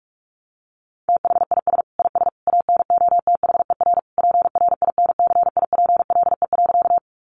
Morse code, heard as a keyed tone, about 44 wpm, 700 Hz. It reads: THIS IS ANOTHER PRINCIPLE.
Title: Sentences from Top 1400 Words 45wpm